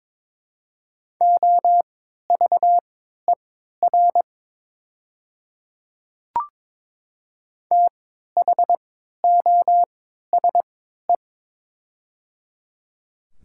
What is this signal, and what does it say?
Morse code, heard as a keyed tone, about 22 wpm, spaced as for 12 wpm, 700 Hz. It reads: OVER THOSE
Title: 3x Character Spacing - Top 300 Words - 22wpm